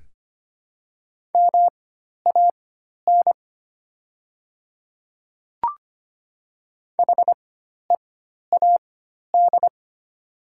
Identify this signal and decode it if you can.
Morse code, heard as a keyed tone, about 25 wpm, spaced as for 12 wpm, 690 Hz. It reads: MAN HEAD
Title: 4x Character Spacing - Top 300 Words - 25wpm